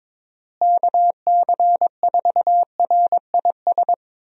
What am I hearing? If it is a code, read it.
Morse code, heard as a keyed tone, about 22 wpm, 700 Hz. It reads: KC4RIS
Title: USA Call Signs - 6 Characters - Repeated 3x - 22wpm